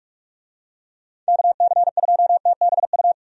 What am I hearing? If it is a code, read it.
Morse code, heard as a keyed tone, about 44 wpm, 700 Hz. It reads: KC2TBU
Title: USA Call Signs - 6 Characters - 45wpm